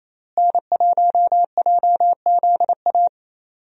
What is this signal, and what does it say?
Morse code, heard as a keyed tone, about 28 wpm, 700 Hz. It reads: N1JZA
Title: USA Call Signs - 5 Characters - Repeated 2x - 28wpm